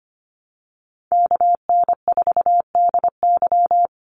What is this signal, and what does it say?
Morse code, heard as a keyed tone, about 25 wpm, 700 Hz. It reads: KN4DY
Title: USA Call Signs 25wpm